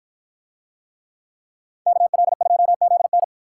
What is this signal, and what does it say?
Morse code, heard as a keyed tone, about 53 wpm, 700 Hz. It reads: KL2ZN